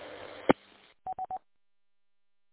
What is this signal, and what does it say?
There is no code.